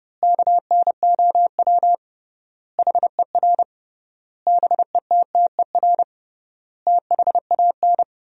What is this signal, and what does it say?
Morse code, heard as a keyed tone, about 32 wpm, 700 Hz. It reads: KNOW HER BETTER THAN
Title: 3x Word-Spacing - Sentences from Top 400 Words - 30wpm